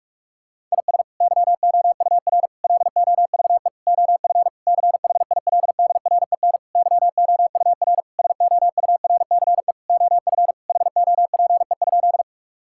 Morse code, heard as a keyed tone, about 41 wpm, 700 Hz. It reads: IS YOUR LOVE OF CHILDREN YOUR SOURCE OF HOPE?
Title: Sentences from Top 1000 Words - Review Part 5 - 45wpm